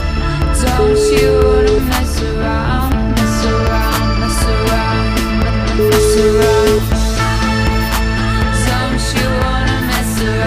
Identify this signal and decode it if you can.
Ringback tone (Europe, 425 Hz), with noise.